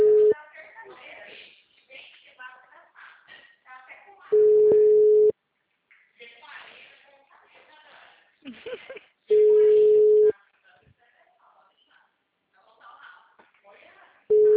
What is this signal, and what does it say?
Ringback tone (Europe, 425 Hz), with speech.